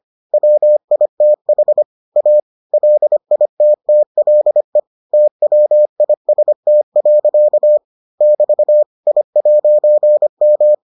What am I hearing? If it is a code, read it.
Morse code, heard as a keyed tone, about 25 wpm, 600 Hz. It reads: WITH A LITTLE TWIST. <BT> I'M